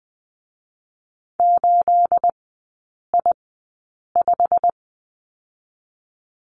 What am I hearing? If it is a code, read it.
Morse code, heard as a keyed tone, about 20 wpm, 700 Hz. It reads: 8 I 5
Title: 212 Lesson 34e - Number 8 - Three Characters 20wpm